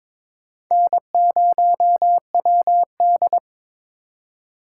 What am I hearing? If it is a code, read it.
Morse code, heard as a keyed tone, about 26 wpm, 700 Hz. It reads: N0WD
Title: Speed Racing - USA Call Signs 22wpm